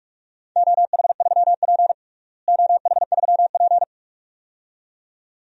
Morse code, heard as a keyed tone, about 44 wpm, 700 Hz. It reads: OH3P OH3P
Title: DXCC-100 Call Signs - 4 Characters - Repeated 2x - 45wpm